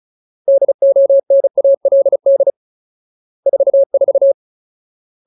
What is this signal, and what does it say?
Morse code, heard as a keyed tone, about 34 wpm, 550 Hz. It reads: DONALD 44
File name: Ultimate Word List / 35WPM Ultimate